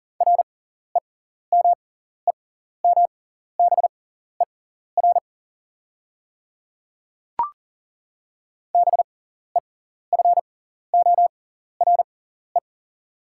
Morse code, heard as a keyed tone, about 39 wpm, spaced as for 14 wpm, 700 Hz. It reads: REMEMBER BEFORE
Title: 6x Character Spacing - Top 500 Words - 40wpm